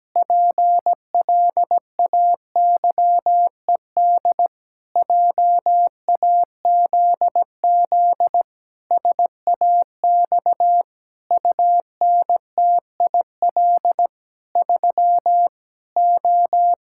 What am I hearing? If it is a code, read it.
Morse code, heard as a keyed tone, about 17 wpm, 700 Hz. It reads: PLAYED JAZZ SAX UNTIL 3 O
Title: Pangram Challenge v2 - 17wpm